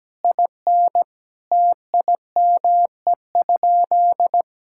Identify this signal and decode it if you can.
Morse code, heard as a keyed tone, about 17 wpm, 700 Hz. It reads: IN TIME?
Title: Sentences from Top 1000 Words - Review Part 5 - 17wpm